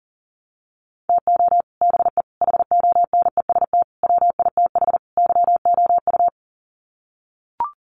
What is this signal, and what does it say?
Morse code, heard as a keyed tone, about 40 wpm, 700 Hz. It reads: TO BE HONEST WITH YOU
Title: Sets of 5 Words Encore 40wpm